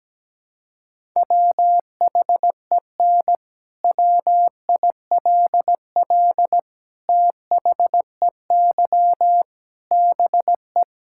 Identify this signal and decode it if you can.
Morse code, heard as a keyed tone, about 17 wpm, 700 Hz. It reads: WHEN WILL THEY BE
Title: Sentences from Top 1000 Words - Review Part 5 - 17wpm